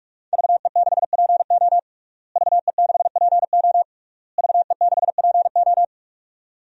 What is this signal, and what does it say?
Morse code, heard as a keyed tone, about 46 wpm, 700 Hz. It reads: VE6PO VE6PO VE6PO
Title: DXCC-100 Call Signs - 5 Characters - Repeated 3x - 45wpm